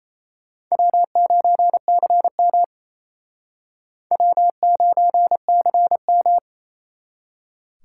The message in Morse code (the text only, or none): W9CM W9CM